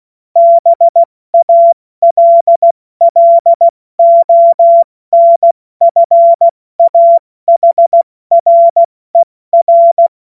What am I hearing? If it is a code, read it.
Morse code, heard as a keyed tone, about 16 wpm, 680 Hz. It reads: BALLONFAHRER
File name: Morsekurs / Witze / Witz 16. August 70 F80